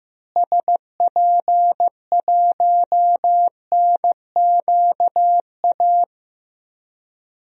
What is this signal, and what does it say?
Morse code, heard as a keyed tone, about 15 wpm, 700 Hz. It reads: SP1NQA
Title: DXCC-100 Call Signs - 6 Characters - Repeated 3x - 15wpm